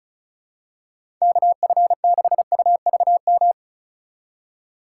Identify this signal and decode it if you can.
Morse code, heard as a keyed tone, about 34 wpm, 690 Hz. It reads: KF6UVM
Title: USA Call Signs - 6 Characters - 2x Character Spacing - 35wpm